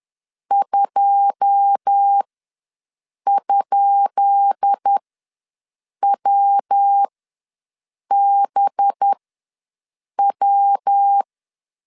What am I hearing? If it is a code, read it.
Morse code, heard as a keyed tone, about 10 wpm, spaced as for 6 wpm, 800 Hz. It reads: 2?WBW
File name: LEZIONI / 10 SIGNS / 10B INT ws 10wpm 4134